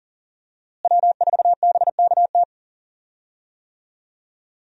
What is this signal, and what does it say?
Morse code, heard as a keyed tone, about 40 wpm, 700 Hz. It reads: W4BKT
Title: USA Call Signs - 5 Characters - Repeated 2x - 40wpm